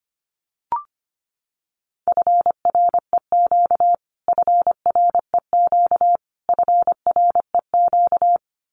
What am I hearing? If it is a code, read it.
Morse code, heard as a keyed tone, about 25 wpm, 700 Hz. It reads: FREQ FREQ FREQ
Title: Abbreviations Repeated 3x - 25wpm